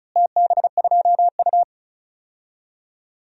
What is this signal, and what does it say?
Morse code, heard as a keyed tone, about 36 wpm, 700 Hz. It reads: TB2U
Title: DXCC-100 Call Signs - 4 Characters - Repeated 3x - 35wpm